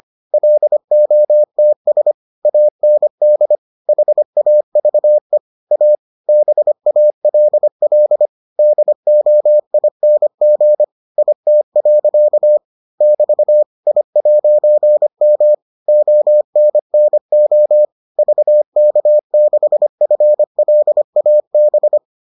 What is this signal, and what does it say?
Morse code, heard as a keyed tone, about 25 wpm, 600 Hz. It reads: LOTS AND HAVE A BALL DOING IT. <BT> I'M ONNO VK6FLAB